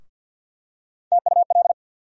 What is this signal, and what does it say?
Morse code, heard as a keyed tone, about 52 wpm, 700 Hz. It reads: TUL
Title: USA Call Sign Postfixes - 3 Letters 50wpm